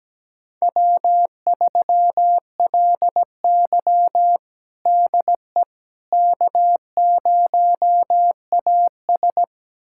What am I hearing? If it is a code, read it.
Morse code, heard as a keyed tone, about 17 wpm, 700 Hz. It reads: W3LY DE K0AS
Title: USA Callsign DE Callsign - 4 Characters - 17wpm